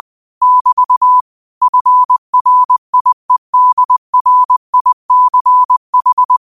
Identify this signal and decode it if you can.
Morse code, heard as a keyed tone, about 20 wpm, 1000 Hz. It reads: <BT> FRIEDRICH
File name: qotdfiles / 20wpm